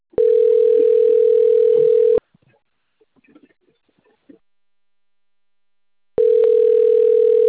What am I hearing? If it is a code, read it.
Ringback tone (North America), 440 + 480 Hz.
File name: out- - - - - .1754